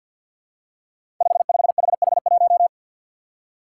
Morse code, heard as a keyed tone, about 47 wpm, 700 Hz. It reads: 55HH1
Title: Five Challenging Letter-Number Combinations 50wpm